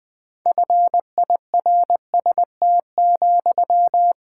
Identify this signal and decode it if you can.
Morse code, heard as a keyed tone, about 21 wpm, 700 Hz. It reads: FIRST,
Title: Sentences from Top 1000 Words - Review Part 6 - 20wpm